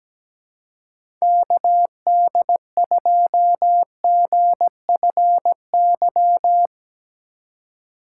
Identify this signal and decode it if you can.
Morse code, heard as a keyed tone, about 17 wpm, 700 Hz. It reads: KD2GFY
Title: USA Call Signs - 6 Characters - 3x Character Spacing - 17wpm